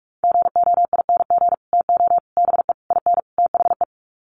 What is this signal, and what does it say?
Morse code, heard as a keyed tone, about 44 wpm, 700 Hz. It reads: GOING TO BE IN THE
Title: Sets of 5 Words Encore 45wpm